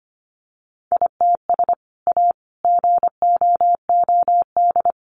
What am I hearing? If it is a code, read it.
Morse code, heard as a keyed tone, about 25 wpm, 700 Hz. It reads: ITS A GOOD
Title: Sentences from Top 600 Words 25wpm